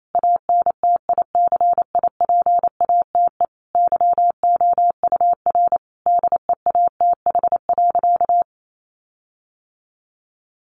Morse code, heard as a keyed tone, about 28 wpm, 700 Hz. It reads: ANTICIPATE YOUR DEATH.
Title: Sentences from Top 1500 Words 28wpm